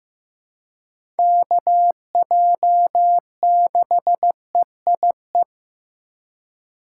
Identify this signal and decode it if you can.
Morse code, heard as a keyed tone, about 15 wpm, 700 Hz. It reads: KJ6EIE
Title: USA Call Signs - 6 Characters - 3x Character Spacing - 15wpm